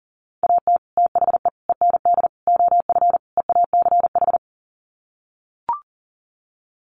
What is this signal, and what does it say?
Morse code, heard as a keyed tone, about 40 wpm, 700 Hz. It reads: AT THE END OF EACH
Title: Sets of 5 Words Encore 40wpm